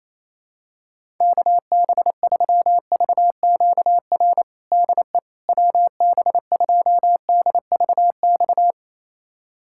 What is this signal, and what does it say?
Morse code, heard as a keyed tone, about 28 wpm, 700 Hz. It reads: KB3VQR DE WB2DVX
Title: USA Callsign DE Callsign - 6 Characters - 28wpm